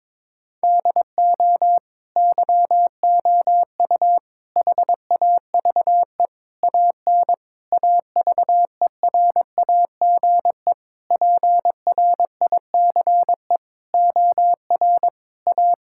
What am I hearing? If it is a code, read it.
Morse code, heard as a keyed tone, about 22 wpm, 700 Hz. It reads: DO YOU HAVE AN AVERAGE PRICE OR A
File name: Sentences with 9-Word Phrases - 22wpm